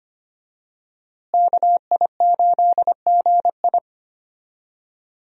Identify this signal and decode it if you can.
Morse code, heard as a keyed tone, about 25 wpm, 690 Hz. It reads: KI8GI
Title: USA Call Signs - 5 Characters - Repeated 2x - 25wpm